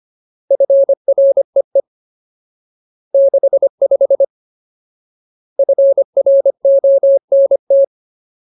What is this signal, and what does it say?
Morse code, heard as a keyed tone, about 25 wpm, 550 Hz. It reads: FREE 65 FRONT
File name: Ultimate Word List / 25WPM Ultimate